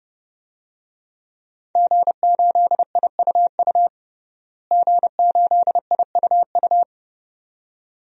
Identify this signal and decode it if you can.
Morse code, heard as a keyed tone, about 31 wpm, 700 Hz. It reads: G8IUU G8IUU
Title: DXCC-100 Call Signs - 5 Characters - Repeated 2x - 30wpm